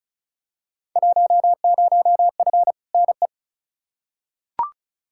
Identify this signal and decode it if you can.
Morse code, heard as a keyed tone, about 36 wpm, 700 Hz. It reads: 10F NE
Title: Field Day Exchanges Encore - 35wpm